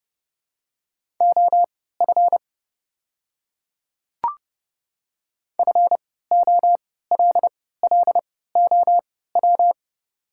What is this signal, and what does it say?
Morse code, heard as a keyed tone, about 30 wpm, spaced as for 17 wpm, 690 Hz. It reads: OF FOLLOW
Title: 3x Character Spacing - Top 200 Words - 30wpm